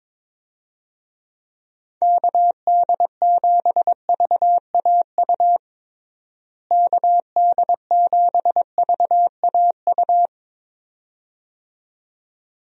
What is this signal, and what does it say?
Morse code, heard as a keyed tone, about 22 wpm, 700 Hz. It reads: KD7VAU KD7VAU
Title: USA Call Signs - 6 Characters - Repeated 2x - 22wpm